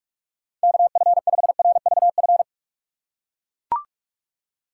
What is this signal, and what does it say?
Morse code, heard as a keyed tone, about 43 wpm, 700 Hz. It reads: KF5RVF
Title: USA Call Signs - 4x Character Spacing - 45wpm